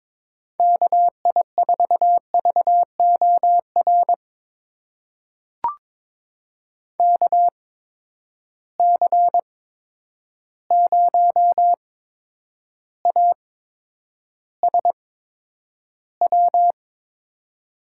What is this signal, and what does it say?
Morse code, heard as a keyed tone, about 22 wpm, spaced as for 9 wpm, 700 Hz. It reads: KI4VOR KC0ASW
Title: USA Call Signs - 6 Characters - 8x Character Spacing - 22wpm